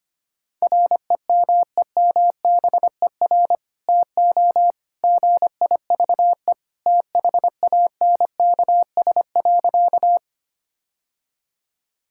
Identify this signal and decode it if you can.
Morse code, heard as a keyed tone, about 25 wpm, 700 Hz. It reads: REMEMBER TO GIVE THANKS.
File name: practice / 25wpm / Sentences with 2-Word Phrases - 25wpm